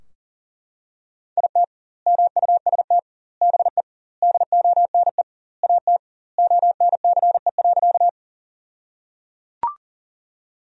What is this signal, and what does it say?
Morse code, heard as a keyed tone, about 37 wpm, 700 Hz. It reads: IT MUST BE DONE AT ONCE.
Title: 2x Word-Spacing - Sentences from Top 500 Words - 40wpm